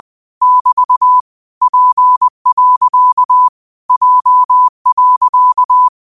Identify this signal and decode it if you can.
Morse code, heard as a keyed tone, about 20 wpm, 1000 Hz. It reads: <BT> P. J.
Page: New quotes every day in morse code at 20 Words per minute.